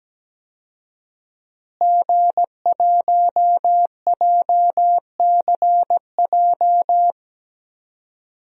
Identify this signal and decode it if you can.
Morse code, heard as a keyed tone, about 17 wpm, 700 Hz. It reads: G1JCJ
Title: DXCC-100 Call Signs - 5 Characters - Repeated 3x - 17wpm